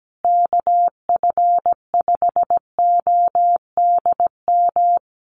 Morse code, heard as a keyed tone, about 17 wpm, 700 Hz. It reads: KF5ODM
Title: USA Call Signs Repeated 3x 17wpm